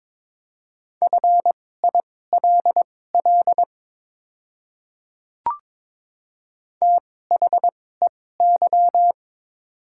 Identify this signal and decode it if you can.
Morse code, heard as a keyed tone, about 22 wpm, spaced as for 16 wpm, 700 Hz. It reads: FILL THEY
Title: 2x Character Spacing - Top 500 Words - 22wpm